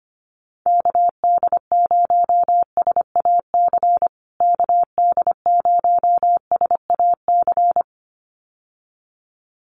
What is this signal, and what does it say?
Morse code, heard as a keyed tone, about 25 wpm, 700 Hz. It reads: KD0SAC KD0SAC
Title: USA Call Signs Repeated 2x 25wpm